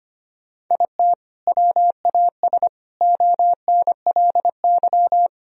Morse code, heard as a keyed tone, about 25 wpm, 700 Hz. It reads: IT WAS ONLY
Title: Sentences from Top 1000 Words - Review Part 1 - 25wpm